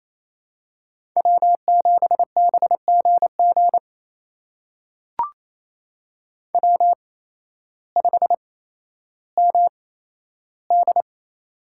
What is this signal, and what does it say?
Morse code, heard as a keyed tone, about 28 wpm, spaced as for 13 wpm, 700 Hz. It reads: W7BGG W5MD
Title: USA Call Signs - 8x Character Spacing - 28wpm